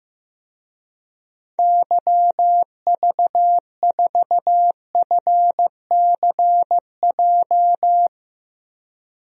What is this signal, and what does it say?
Morse code, heard as a keyed tone, about 15 wpm, 700 Hz. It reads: YV4FCJ
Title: DXCC-100 Call Signs - 6 Characters - Repeated 3x - 15wpm